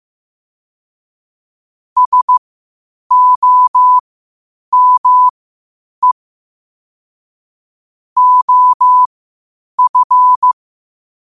Morse code, heard as a keyed tone, about 15 wpm, spaced as for 8 wpm, 1000 Hz. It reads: SOME OF
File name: qotdfiles / 8wpm